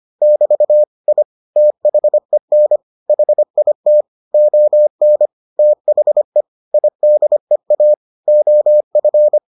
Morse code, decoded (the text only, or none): <BT> I THEN HIT ON THE IDEA OF